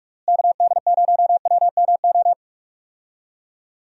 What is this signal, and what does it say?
Morse code, heard as a keyed tone, about 46 wpm, 700 Hz. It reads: KD0WMO
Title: USA Call Signs - 6 Characters - Repeated 3x - 45wpm